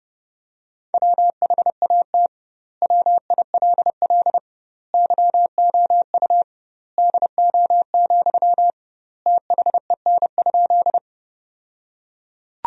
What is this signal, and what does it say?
Morse code, heard as a keyed tone, about 29 wpm, 700 Hz. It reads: WHAT WILL YOU DO, THEN?
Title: 2x Word-Spacing - Sentences from Top 100 Words - 30wpm